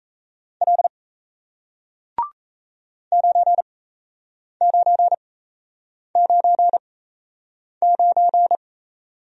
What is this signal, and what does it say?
Morse code, heard as a keyed tone, about 37 wpm, spaced as for 8 wpm, 700 Hz. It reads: L 9999